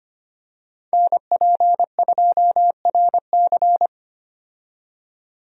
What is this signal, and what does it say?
Morse code, heard as a keyed tone, about 25 wpm, 700 Hz. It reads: NP2RC
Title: USA Call Signs - 5 Characters - 25wpm